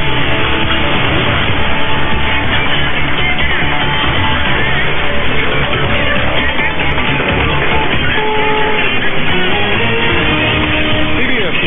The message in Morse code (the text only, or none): none